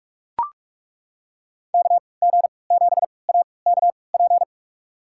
Morse code, heard as a keyed tone, about 42 wpm, spaced as for 25 wpm, 700 Hz. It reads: KG7AKP
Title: USA Call Signs - 3x Character Spacing - 45wpm